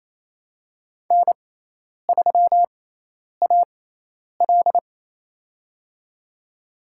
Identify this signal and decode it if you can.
Morse code, heard as a keyed tone, about 28 wpm, spaced as for 10 wpm, 700 Hz. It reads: N3AL
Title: USA Call Signs - 6x Character Spacing - 28wpm